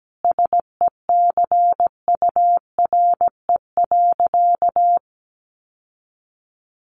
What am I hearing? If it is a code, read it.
Morse code, heard as a keyed tone, about 17 wpm, 700 Hz. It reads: SECURE.
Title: Sentences from Top 1500 Words 17wpm